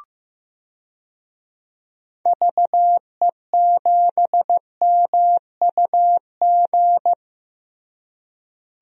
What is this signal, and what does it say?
Morse code, heard as a keyed tone, about 16 wpm, 700 Hz. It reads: VE7MUG
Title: DXCC-100 Call Signs - 6 Characters - Repeated 3x - 15wpm